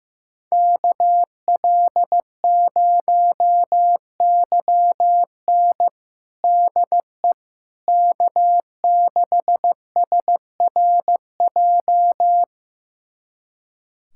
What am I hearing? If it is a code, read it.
Morse code, heard as a keyed tone, about 15 wpm, 700 Hz. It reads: KL0YN DE K6SRJ
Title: USA Callsign DE Callsign - 5 Characters - 15wpm